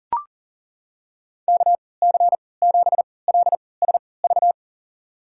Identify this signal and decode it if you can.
Morse code, heard as a keyed tone, about 37 wpm, spaced as for 22 wpm, 700 Hz. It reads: KC7LSV
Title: USA Call Signs - 6 Characters - 3x Character Spacing - 40wpm